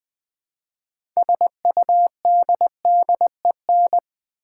Morse code, heard as a keyed tone, about 20 wpm, 700 Hz. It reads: SUDDEN
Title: Build A Word - 3 Letters At A Time - 6-Letter Words - 20wpm